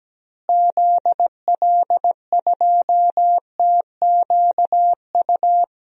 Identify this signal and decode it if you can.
Morse code, heard as a keyed tone, about 17 wpm, 700 Hz. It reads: ZL2TQU
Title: DXCC-100 Call Signs - 6 Characters - Repeated 2x - 17wpm